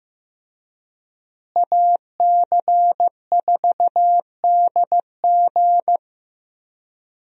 Morse code, heard as a keyed tone, about 15 wpm, 700 Hz. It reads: AC4DG
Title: USA Call Signs - 5 Characters - Repeated 3x - 15wpm